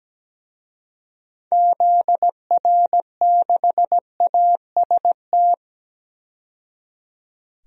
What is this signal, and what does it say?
Morse code, heard as a keyed tone, about 17 wpm, 700 Hz. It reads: ZR6AST
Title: DXCC-100 Call Signs - 6 Characters - 17wpm